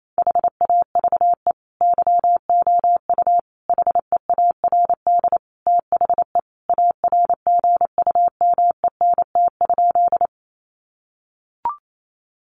Morse code, heard as a keyed tone, about 28 wpm, 700 Hz. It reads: HAVE YOU HEARD THE ARGUMENT?
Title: Sentences from Top 1400 Words 28wpm